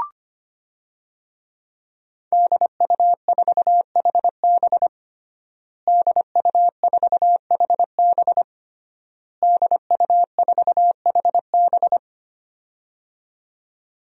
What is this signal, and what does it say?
Morse code, heard as a keyed tone, about 25 wpm, 700 Hz. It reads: DU4HB DU4HB DU4HB
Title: DXCC-100 Call Signs - 5 Characters - Repeated 3x - 25wpm